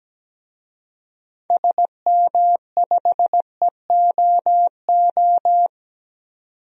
Morse code, heard as a keyed tone, about 17 wpm, 700 Hz. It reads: SM5EOO
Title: DXCC-100 Call Signs - 6 Characters - 17wpm